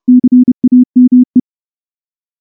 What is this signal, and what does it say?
Morse code, heard as a keyed tone, about 29 wpm, 250 Hz. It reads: CAME
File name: morse / morse-training-30WPM